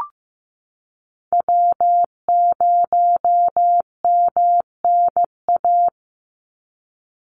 Morse code, heard as a keyed tone, about 15 wpm, 700 Hz. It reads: W0MNA
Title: NA SOTA Chaser Callsigns 15wpm